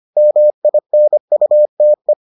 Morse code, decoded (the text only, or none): MINUTE